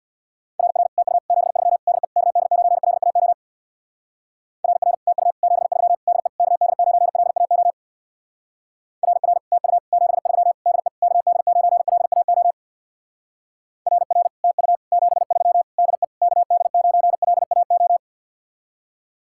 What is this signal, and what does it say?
Morse code, heard as a keyed tone, about 61 wpm, 700 Hz. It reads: RR TU 73 DE KD9LAO RR TU 73 DE KD9LAO RR TU 73 DE KD9LAO RR TU 73 DE KD9LAO
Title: Speed-Racing Common QSO Phrases Encore - 50wpm